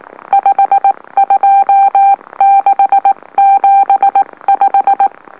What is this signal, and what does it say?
Morse code, heard as a keyed tone, about 18 wpm, 790 Hz. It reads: 52675
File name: M04- -2345utc-LOLO